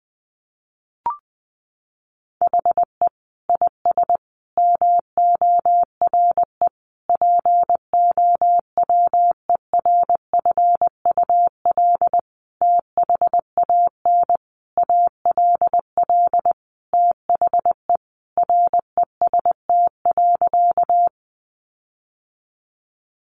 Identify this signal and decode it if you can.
Morse code, heard as a keyed tone, about 20 wpm, 700 Hz. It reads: HE IS MORE POWERFUL THAN ALL THE REST.
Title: Sentences from Top 1300 Words 20wpm